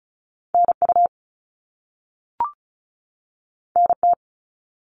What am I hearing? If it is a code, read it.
Morse code, heard as a keyed tone, about 36 wpm, 700 Hz. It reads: NU NT
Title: Canadian Provinces and Territories 35wpm